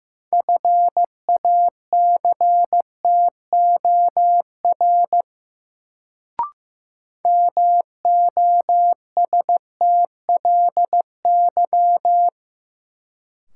Morse code, heard as a keyed tone, about 15 wpm, 700 Hz. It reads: FACTOR MOSTLY
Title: Sentences from Top 1300 Words - New Word Drill Alt - 15wpm